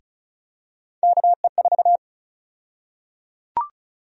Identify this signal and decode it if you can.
Morse code, heard as a keyed tone, about 36 wpm, 700 Hz. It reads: KE4
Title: USA Call Sign Prefixes - 35wpm